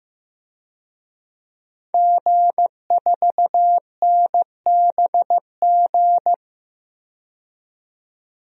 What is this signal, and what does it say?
Morse code, heard as a keyed tone, about 15 wpm, 700 Hz. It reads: G4NBG
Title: DXCC-100 Call Signs - 5 Characters - 15wpm